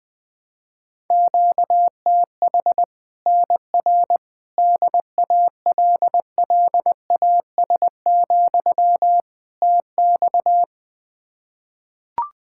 Morse code, heard as a keyed tone, about 20 wpm, 700 Hz. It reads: QTH NR DALLAS, TX
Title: Practice QSO - Exchange 1 - Beginning - 20wpm